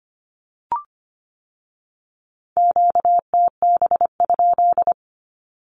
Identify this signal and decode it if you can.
Morse code, heard as a keyed tone, about 25 wpm, 700 Hz. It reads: QTB?
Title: Q-Codes with Spoken Meaning - 25wpm